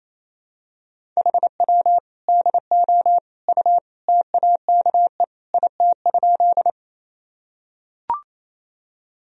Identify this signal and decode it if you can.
Morse code, heard as a keyed tone, about 28 wpm, 700 Hz. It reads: HW DO U TAKE IT?
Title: Sentences from Top 200 Words and Abbreviations 28wpm